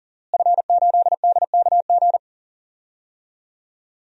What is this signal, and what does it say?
Morse code, heard as a keyed tone, about 37 wpm, 700 Hz. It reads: F8DKG
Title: DXCC-100 Call Signs - 5 Characters - Repeated 2x - 40wpm